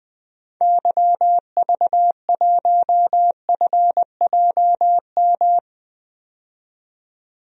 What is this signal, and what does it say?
Morse code, heard as a keyed tone, about 20 wpm, 700 Hz. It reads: YV1FJM
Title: DXCC-100 Call Signs - 6 Characters - 20wpm